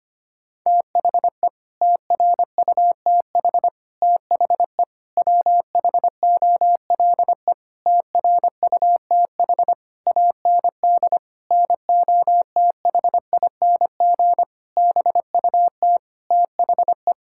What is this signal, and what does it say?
Morse code, heard as a keyed tone, about 25 wpm, 700 Hz. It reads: THE TRUTH THE WHOLE TRUTH AND NOTHING BUT THE
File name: Sets of 9 Words - 25wpm